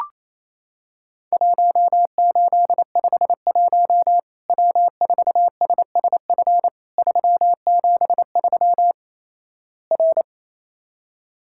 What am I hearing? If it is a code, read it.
Morse code, heard as a keyed tone, about 28 wpm, 700 Hz. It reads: 1851 W4SSF 373 R
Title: Worked All Europe DX Contest - QTC Exchange - 28wpm